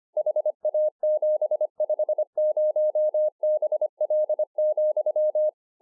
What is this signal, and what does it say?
Morse code, heard as a keyed tone, about 25 wpm, 610 Hz. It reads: HA750BL,
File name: 2011-04-mp3 / BEGINNERS DIARY 25WPM